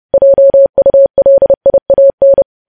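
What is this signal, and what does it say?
Morse code, heard as a keyed tone, about 30 wpm, 550 Hz. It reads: JULIAN
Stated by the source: Morse code news headlines sent at 30 words per minute.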